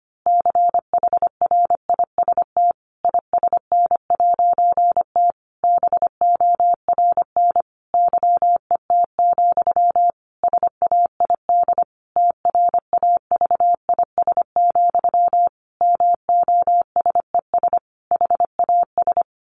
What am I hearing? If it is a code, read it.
Morse code, heard as a keyed tone, about 25 wpm, 700 Hz. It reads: CHRIST ISN'T BORN YET, SAID TRAVIS, MOSES HAS
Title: A Sound Of Thunder 25wpm